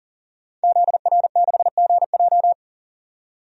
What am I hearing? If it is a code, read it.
Morse code, heard as a keyed tone, about 40 wpm, 700 Hz. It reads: ZR6GJ
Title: DXCC-100 Call Signs - 5 Characters - 40wpm